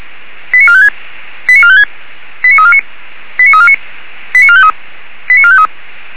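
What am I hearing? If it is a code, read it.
Five-tone selective call (ZVEI), heard as quick runs of five tones, tones of about 70 ms. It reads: 89456 89456 89349 89349 89453 89453
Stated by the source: ZVEI five-tone selective call received by SDR.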